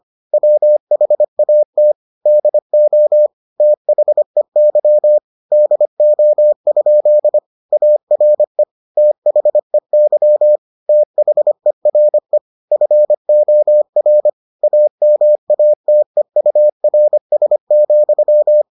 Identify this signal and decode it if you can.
Morse code, heard as a keyed tone, about 25 wpm, 600 Hz. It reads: WHAT DO THEY DO? ARE THEY THERE FOR AMATEURS,